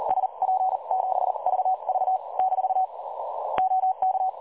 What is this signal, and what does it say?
Morse code, heard as a keyed tone, about 36 wpm, spaced as for 31 wpm, 750 Hz. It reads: SP7VV - OK
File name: dlazaawansowanych / nagraniazpasma / websdr recording 3565.0kHz